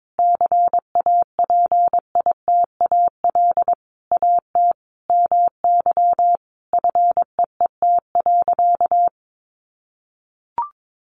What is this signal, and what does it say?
Morse code, heard as a keyed tone, about 22 wpm, 700 Hz. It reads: CAPITAL AT MY FEET.
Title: Sentences from Top 1000 Words 22wpm